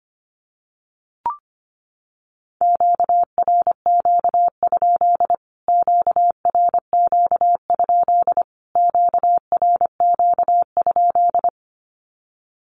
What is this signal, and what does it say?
Morse code, heard as a keyed tone, about 25 wpm, 700 Hz. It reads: QRQ? QRQ? QRQ?
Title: Q-Codes Repeated 3x 25wpm